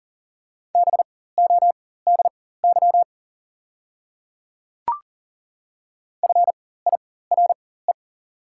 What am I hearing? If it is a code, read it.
Morse code, heard as a keyed tone, about 40 wpm, spaced as for 19 wpm, 690 Hz. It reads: BODY FIRE
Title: 4x Character Spacing - Top 500 Words - 40wpm